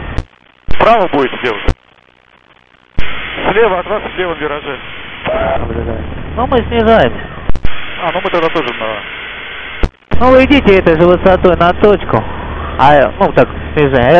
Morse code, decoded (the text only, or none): none